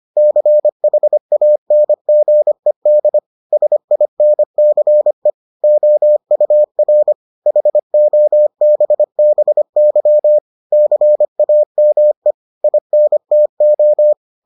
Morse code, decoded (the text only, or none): CHANGED SINCE OUR HOBBY CAME INTO